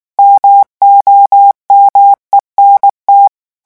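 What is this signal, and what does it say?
Morse code, heard as a keyed tone, about 19 wpm, 800 Hz. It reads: MOMENT
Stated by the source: French Morse revision 8 drill of random five-letter groups at 18 words per minute.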